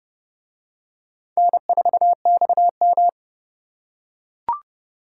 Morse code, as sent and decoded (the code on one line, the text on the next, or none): -. ....- -..- --
N4XM